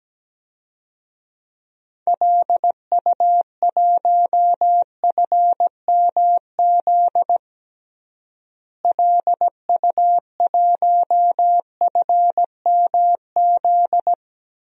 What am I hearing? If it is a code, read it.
Morse code, heard as a keyed tone, about 17 wpm, 700 Hz. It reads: LU1FMZ LU1FMZ
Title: DXCC-100 Call Signs - 6 Characters - Repeated 3x - 17wpm